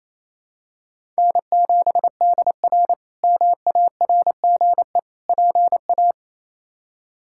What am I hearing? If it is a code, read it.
Morse code, heard as a keyed tone, about 28 wpm, 700 Hz. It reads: N7DR MARGE PA
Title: K1USN SST QSO Practice - 28wpm